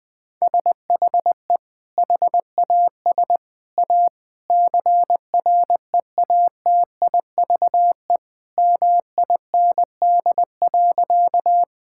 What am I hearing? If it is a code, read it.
Morse code, heard as a keyed tone, about 21 wpm, 700 Hz. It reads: SHE HAS A CREATIVE MIND.
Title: Sentences from Top 1000 Words - Review Part 5 - 20wpm